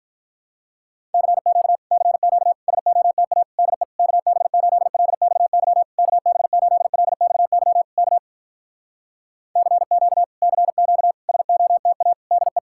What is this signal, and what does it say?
Morse code, heard as a keyed tone, about 46 wpm, 700 Hz. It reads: CQ CQ SOTA DE KB8LXY KB8LXY K CQ CQ SOTA DE